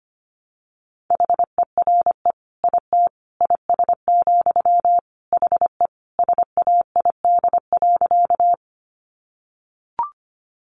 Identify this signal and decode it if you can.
Morse code, heard as a keyed tone, about 25 wpm, 700 Hz. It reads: HERE IT IS, HE SAID.
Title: Sentences from Top 200 Words 25wpm